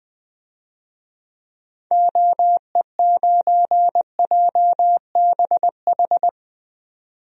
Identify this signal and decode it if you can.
Morse code, heard as a keyed tone, about 20 wpm, 700 Hz. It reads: OE9JBH
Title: DXCC-100 Call Signs - 6 Characters - 20wpm